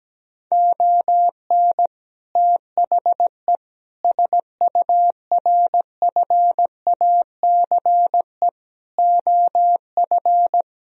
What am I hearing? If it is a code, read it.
Morse code, heard as a keyed tone, about 17 wpm, 700 Hz. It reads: ON THE SURFACE OF
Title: Sentences from Top 1000 Words - Review Part 3 - 17wpm